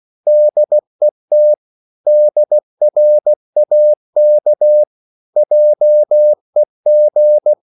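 Morse code, heard as a keyed tone, about 16 wpm, 600 Hz. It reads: DET DRAK JEG
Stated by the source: Slow (QRS) Morse code at 16 wpm, from CW med Gnister.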